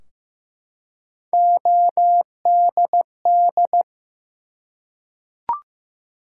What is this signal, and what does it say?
Morse code, heard as a keyed tone, about 15 wpm, 700 Hz. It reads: ODD
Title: Three Letter Words - 15wpm